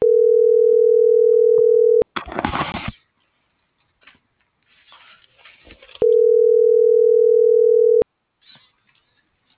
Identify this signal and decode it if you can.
Ringback tone (North America), 440 + 480 Hz.